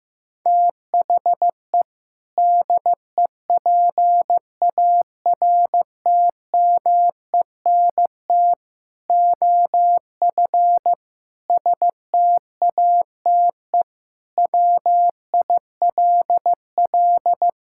Morse code, heard as a keyed tone, about 15 wpm, 700 Hz. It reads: THE DEPARTMENT OF STATE WILL
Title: Sentences from Top 1000 Words - Review Part 5 - 15wpm